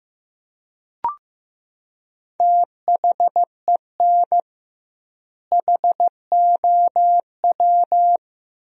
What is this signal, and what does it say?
Morse code, heard as a keyed tone, about 15 wpm, 700 Hz. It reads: THEN HOW
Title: 2x Word-Spacing - Sentences from Top 100 Words - 15wpm